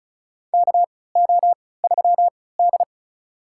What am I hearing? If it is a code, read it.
Morse code, heard as a keyed tone, about 34 wpm, spaced as for 20 wpm, 700 Hz. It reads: KO3D
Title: USA Call Signs - 4 Characters - 3x Character Spacing - 35wpm